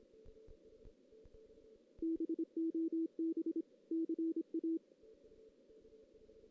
Morse code, heard as a keyed tone, about 26 wpm, 330 Hz. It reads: BOB CA
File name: 2016 JAN CW NAQP / 2016 Jan naqp k5ot - 0-3